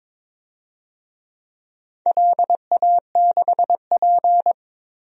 Morse code, heard as a keyed tone, about 23 wpm, 700 Hz. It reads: LA6P
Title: DXCC-100 Call Signs - 4 Characters - 22wpm